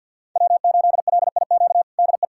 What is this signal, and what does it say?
Morse code, heard as a keyed tone, about 50 wpm, 700 Hz. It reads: W8LIQ DE
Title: USA Callsign DE Callsign 50wpm